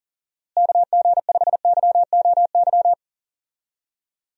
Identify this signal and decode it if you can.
Morse code, heard as a keyed tone, about 40 wpm, 700 Hz. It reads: KG5YOY
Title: USA Call Signs - 6 Characters - Repeated 3x - 40wpm